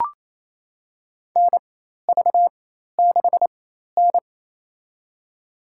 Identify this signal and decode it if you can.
Morse code, heard as a keyed tone, about 28 wpm, spaced as for 13 wpm, 700 Hz. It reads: NV6N
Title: USA Call Signs - 4x Character Spacing - 28wpm